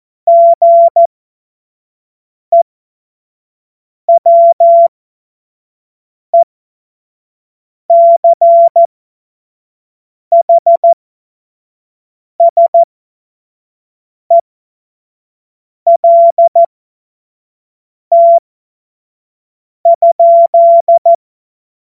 Morse code, heard as a keyed tone, about 14 wpm, spaced as for 5 wpm, 680 Hz. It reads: GEWECHSELT?
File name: Morsekurs / Witze / Witz 29. Juni 2021, 25 F70